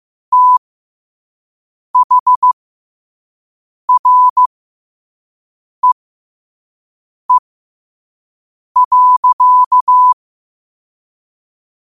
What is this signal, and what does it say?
Morse code, heard as a keyed tone, about 15 wpm, spaced as for 5 wpm, 1000 Hz. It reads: THREE.
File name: qotdfiles / 5wpm